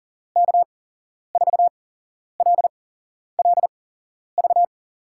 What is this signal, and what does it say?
Morse code, heard as a keyed tone, about 42 wpm, spaced as for 11 wpm, 700 Hz. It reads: K4LLV
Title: USA Call Signs - 8x Character Spacing - 40wpm